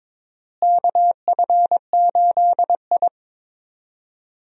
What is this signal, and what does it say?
Morse code, heard as a keyed tone, about 22 wpm, 700 Hz. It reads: KF8I
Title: USA Call Signs - 4 Characters - Repeated 2x - 22wpm